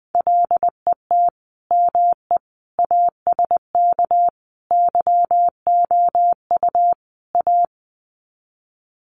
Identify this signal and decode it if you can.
Morse code, heard as a keyed tone, about 20 wpm, 700 Hz. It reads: LET ME ASK YOU A
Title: Sets of 5 Words Encore 20wpm